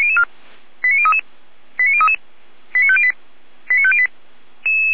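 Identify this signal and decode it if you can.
Five-tone selective call (ZVEI), heard as quick runs of five tones, tones of about 80 ms. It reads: 89033 89033 89508 89508
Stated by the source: ZVEI five-tone call 89003 received by SDR.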